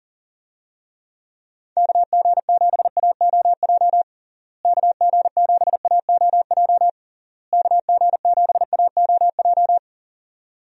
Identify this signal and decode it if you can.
Morse code, heard as a keyed tone, about 40 wpm, 700 Hz. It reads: KG7AOJ KG7AOJ KG7AOJ
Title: USA Call Signs - 6 Characters - Repeated 3x - 40wpm